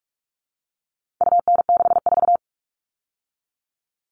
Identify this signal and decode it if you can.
Morse code, heard as a keyed tone, about 44 wpm, 700 Hz. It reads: UN64
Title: All Cut Numbers - 4-Digit 45wpm